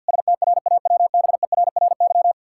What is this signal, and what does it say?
Morse code, heard as a keyed tone, about 50 wpm, 700 Hz. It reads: STRAWBERRY
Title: Pangram Challenge v2 - 50wpm